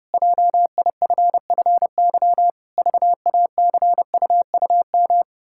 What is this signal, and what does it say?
Morse code, heard as a keyed tone, about 30 wpm, 700 Hz. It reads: JIFFY VACUUM
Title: Pangram Challenge v2 - 30wpm